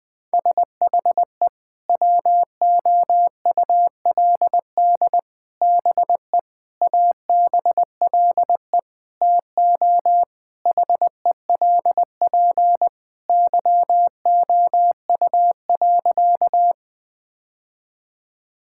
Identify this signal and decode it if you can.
Morse code, heard as a keyed tone, about 20 wpm, 700 Hz. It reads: SHE WOULD BE ABLE TO HELP YOU.
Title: Sentences with 4-Word Phrases - 20wpm